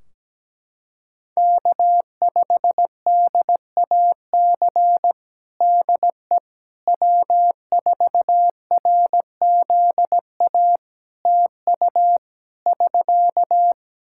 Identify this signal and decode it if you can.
Morse code, heard as a keyed tone, about 17 wpm, 700 Hz. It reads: K5DAC DE W4RZA TU <SK>
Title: Practice QSO - Exchange 4 - Ending - 17wpm